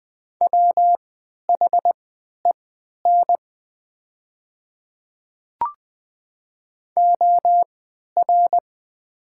Morse code, heard as a keyed tone, about 20 wpm, spaced as for 11 wpm, 700 Hz. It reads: WHEN OR
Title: 3x Character Spacing - Top 500 Words - 20wpm